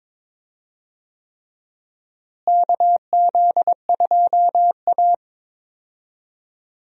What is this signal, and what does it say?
Morse code, heard as a keyed tone, about 22 wpm, 700 Hz. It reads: KZ2A
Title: USA Call Signs - 4 Characters - Repeated 3x - 22wpm